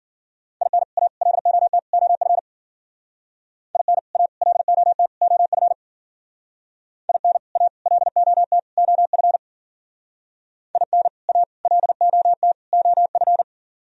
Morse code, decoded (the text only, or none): IN A LOT OF IN A LOT OF IN A LOT OF IN A LOT OF